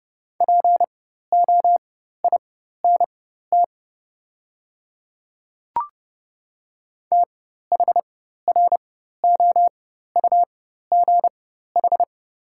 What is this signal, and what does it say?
Morse code, heard as a keyed tone, about 32 wpm, spaced as for 14 wpm, 700 Hz. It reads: POINT THROUGH
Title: 4x Character Spacing - Top 300 Words - 30wpm